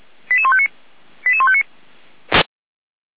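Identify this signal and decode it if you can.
Five-tone selective call (ZVEI), heard as quick runs of five tones, tones of about 70 ms. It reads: 88159 88159
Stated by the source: ZVEI five-tone call received on an SDR.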